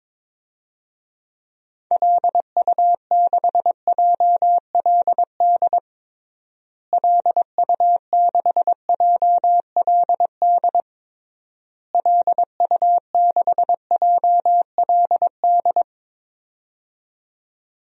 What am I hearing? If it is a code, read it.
Morse code, heard as a keyed tone, about 22 wpm, 700 Hz. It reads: LU6JLD LU6JLD LU6JLD
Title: DXCC-100 Call Signs - 6 Characters - Repeated 3x - 22wpm